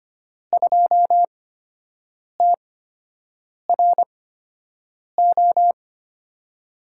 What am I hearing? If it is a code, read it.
Morse code, heard as a keyed tone, about 25 wpm, spaced as for 7 wpm, 700 Hz. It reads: 2TRO
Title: USA Call Signs - 6 Characters - 8x Character Spacing - 25wpm